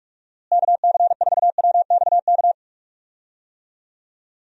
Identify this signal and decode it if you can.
Morse code, heard as a keyed tone, about 43 wpm, 700 Hz. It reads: KC4WXK
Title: USA Call Signs - 6 Characters - Repeated 3x - 45wpm